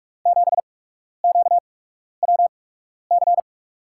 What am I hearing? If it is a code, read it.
Morse code, heard as a keyed tone, about 46 wpm, spaced as for 12 wpm, 700 Hz. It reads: 7QWC
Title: USA Call Signs - 6 Characters - 8x Character Spacing - 45wpm